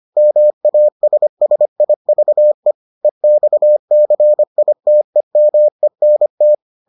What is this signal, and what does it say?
Morse code, heard as a keyed tone, about 25 wpm, 600 Hz. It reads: MASSIVE EXCITEMENT